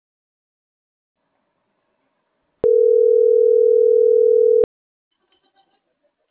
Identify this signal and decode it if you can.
Ringback tone (North America), 440 + 480 Hz.